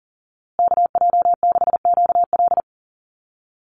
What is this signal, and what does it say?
Morse code, heard as a keyed tone, about 40 wpm, 700 Hz. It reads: KJ6QL
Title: USA Call Signs Repeated 3x 40wpm